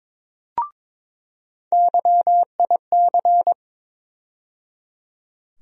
Morse code, heard as a keyed tone, about 21 wpm, 700 Hz. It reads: YIC
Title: USA Call Sign Postfixes - 3 Letters 22wpm